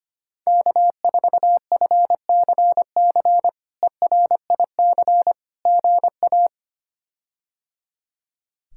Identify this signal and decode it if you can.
Morse code, heard as a keyed tone, about 25 wpm, 700 Hz. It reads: K4FCC ERIC GA
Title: K1USN SST QSO Practice - 25wpm